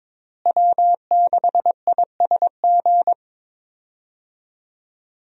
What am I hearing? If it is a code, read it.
Morse code, heard as a keyed tone, about 22 wpm, 700 Hz. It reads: W6ISG
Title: USA Call Signs - 5 Characters - Repeated 2x - 22wpm